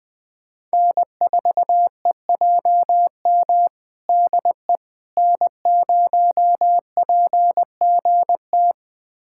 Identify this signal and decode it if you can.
Morse code, heard as a keyed tone, about 20 wpm, 700 Hz. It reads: N4EJM DE N0PGT
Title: USA Callsign DE Callsign - 5 Characters - 20wpm